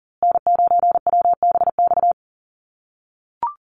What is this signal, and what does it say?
Morse code, heard as a keyed tone, about 37 wpm, 700 Hz. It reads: N9WBX
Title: USA Call Signs Repeated 2x 40wpm